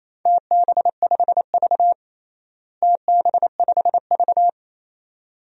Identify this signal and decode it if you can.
Morse code, heard as a keyed tone, about 28 wpm, 700 Hz. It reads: TB5V TB5V
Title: DXCC-100 Call Signs - 4 Characters - Repeated 2x - 28wpm